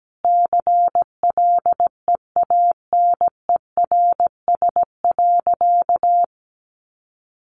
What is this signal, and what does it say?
Morse code, heard as a keyed tone, about 17 wpm, 700 Hz. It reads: CLEANERS.
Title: Additional Easy Phrases 17wpm